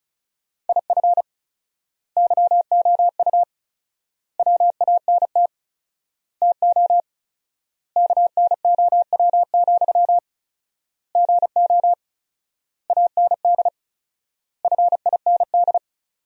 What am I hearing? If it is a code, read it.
Morse code, heard as a keyed tone, about 34 wpm, 700 Hz. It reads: IF YOU WANT TO KNOW, GO AND FIND
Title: 4x Word-Spacing - Sentences from Top 200 Words - 35wpm